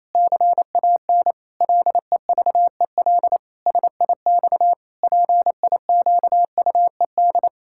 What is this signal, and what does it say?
Morse code, heard as a keyed tone, about 28 wpm, 700 Hz. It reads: CAN LEVEL SIX PIQUED
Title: Pangram Challenge v2 - 28wpm